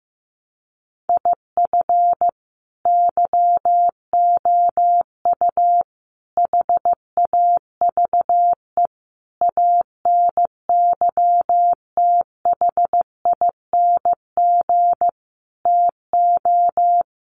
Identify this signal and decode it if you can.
Morse code, heard as a keyed tone, about 15 wpm, 700 Hz. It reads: IF YOU HAVE ANYTHING TO
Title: Sentences from Top 1000 Words 15wpm